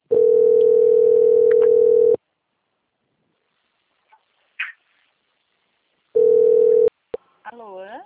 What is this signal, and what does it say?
Ringback tone (North America), 440 + 480 Hz, with speech.
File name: out- - - - - .19087